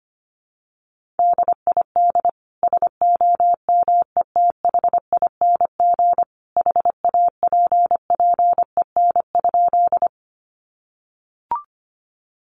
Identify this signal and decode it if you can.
Morse code, heard as a keyed tone, about 25 wpm, 700 Hz. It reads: DID SOMETHING HAPPEN?
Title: Sentences from Top 600 Words 25wpm